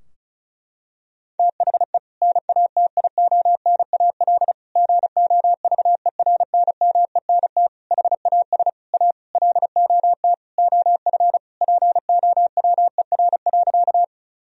Morse code, heard as a keyed tone, about 34 wpm, 700 Hz. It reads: THE NATIONAL GOVERNMENT HAS A LOT OF POWER.
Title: Sentences from Top 1000 Words - Review Part 4 - 35wpm